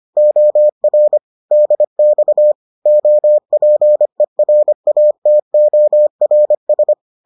Morse code, heard as a keyed tone, about 25 wpm, 600 Hz. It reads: OR DX OPERATORS